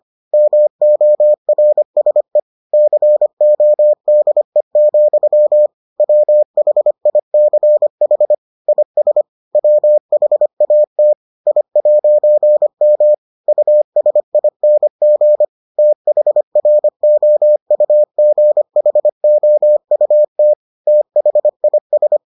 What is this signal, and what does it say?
Morse code, heard as a keyed tone, about 25 wpm, 600 Hz. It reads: MORSE CODE, WHICH IS WHAT I'M USING THROUGHOUT THIS